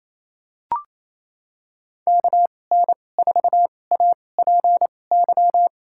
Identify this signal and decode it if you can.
Morse code, heard as a keyed tone, about 28 wpm, spaced as for 20 wpm, 700 Hz. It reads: KN4APY
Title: USA Call Signs - 2x Character Spacing - 28wpm